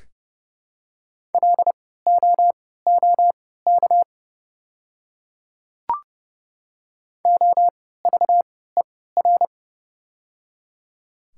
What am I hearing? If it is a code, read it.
Morse code, heard as a keyed tone, about 30 wpm, spaced as for 17 wpm, 700 Hz. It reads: LOOK OVER
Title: 3x Character Spacing - Top 100 Words - 30wpm